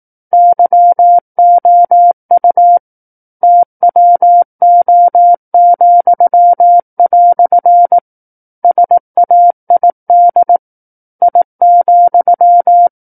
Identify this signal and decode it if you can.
Morse code, heard as a keyed tone, about 18 wpm, 700 Hz. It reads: YOU TWO," SAID I,
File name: 18 WPM